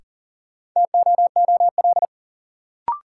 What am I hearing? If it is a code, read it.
Morse code, heard as a keyed tone, about 40 wpm, 700 Hz. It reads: TOOL
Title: Sentences from Top 1000 Words - Review Part 6 - New Word Drill - 40wpm